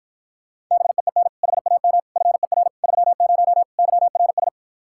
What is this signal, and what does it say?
Morse code, heard as a keyed tone, about 54 wpm, 700 Hz. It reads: BEEN HAM FER 30 YRS